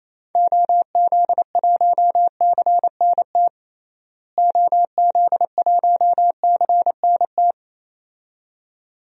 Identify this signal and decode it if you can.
Morse code, heard as a keyed tone, about 28 wpm, 700 Hz. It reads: OZ1CNT OZ1CNT
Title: DXCC-100 Call Signs - 6 Characters - Repeated 2x - 28wpm